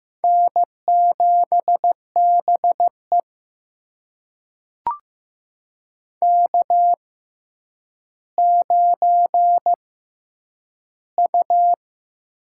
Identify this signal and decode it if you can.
Morse code, heard as a keyed tone, about 15 wpm, spaced as for 9 wpm, 700 Hz. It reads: N7BE K9U
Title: USA Call Signs - 4 Characters - 6x Character Spacing - 15wpm